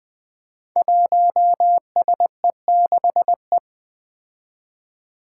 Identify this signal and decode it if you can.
Morse code, heard as a keyed tone, about 20 wpm, 700 Hz. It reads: 1SE6E
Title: Five Challenging Letter-Number Combinations 20wpm